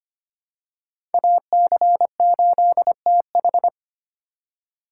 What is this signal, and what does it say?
Morse code, heard as a keyed tone, about 25 wpm, 700 Hz. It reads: AC8TH
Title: USA Call Signs - 5 Characters - 2x Character Spacing - 25wpm